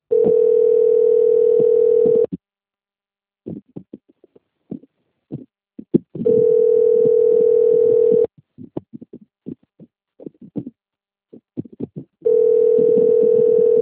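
Ringback tone (North America), 440 + 480 Hz.
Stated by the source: Outbound telephone call from Mogi Guacu, SP.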